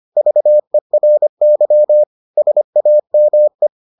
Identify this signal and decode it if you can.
Morse code, heard as a keyed tone, about 25 wpm, 600 Hz. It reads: VERY SAME